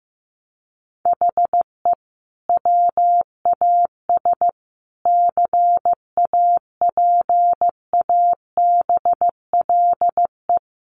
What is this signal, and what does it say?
Morse code, heard as a keyed tone, about 15 wpm, 700 Hz. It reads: HE WAS CAPABLE
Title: Sentences from Top 1200 Words 15wpm